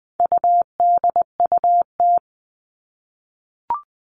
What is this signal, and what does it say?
Morse code, heard as a keyed tone, about 20 wpm, 700 Hz. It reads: UDUT
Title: All Cut Numbers - 4-Digit 20wpm